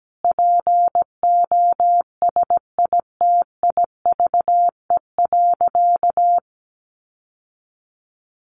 Morse code, heard as a keyed tone, about 17 wpm, 700 Hz. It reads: POSITIVE.
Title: Sentences from Top 1400 Words 17wpm